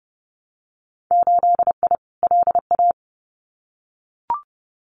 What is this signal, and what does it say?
Morse code, heard as a keyed tone, about 30 wpm, 700 Hz. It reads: 8I LA
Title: Winter Field Day Exchanges Encore 30wpm